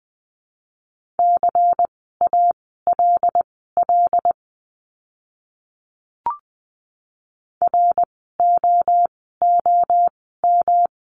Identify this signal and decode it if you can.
Morse code, heard as a keyed tone, about 20 wpm, spaced as for 15 wpm, 700 Hz. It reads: CALL ROOM
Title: 115 Lesson 19f - Letter w - Words 20wpm